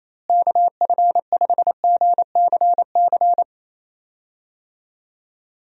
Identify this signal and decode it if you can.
Morse code, heard as a keyed tone, about 27 wpm, 700 Hz. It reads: KF5GCC
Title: USA Call Signs - 6 Characters - Repeated 2x - 28wpm